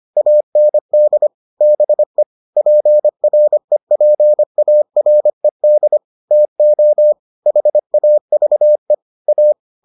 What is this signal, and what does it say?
Morse code, heard as a keyed tone, about 25 wpm, 600 Hz. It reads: AND BE PREPARED TO HAVE A